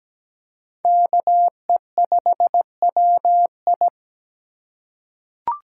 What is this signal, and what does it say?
Morse code, heard as a keyed tone, about 17 wpm, 700 Hz. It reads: KE5WI
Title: USA Call Signs - 5 Characters - 4x Character Spacing - 17wpm